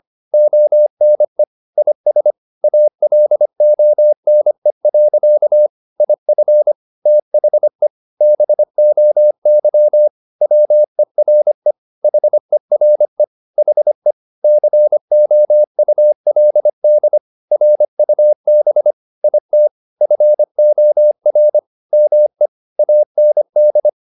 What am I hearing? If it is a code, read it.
Morse code, heard as a keyed tone, about 25 wpm, 600 Hz. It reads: ONE IS ALONE. IF THE BOY WERE HERE HE COULD RUB IT FOR ME AND
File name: the old man and the sea part 1 25wpm